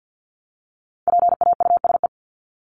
Morse code, heard as a keyed tone, about 50 wpm, 700 Hz. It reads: PAUSE
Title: Top 1501-1750 Words 50wpm